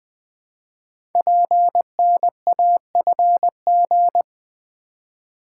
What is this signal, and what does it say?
Morse code, heard as a keyed tone, about 20 wpm, 700 Hz. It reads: PNAFG
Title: Code Groups - 5 Letters - 20wpm